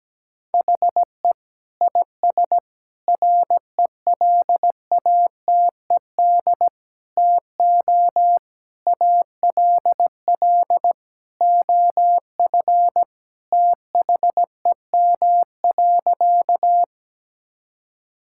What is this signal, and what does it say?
Morse code, heard as a keyed tone, about 17 wpm, 700 Hz. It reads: HE IS RELATED TO ALL OF THEM.
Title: Sentences from Top 1000 Words - Review Part 5 - 17wpm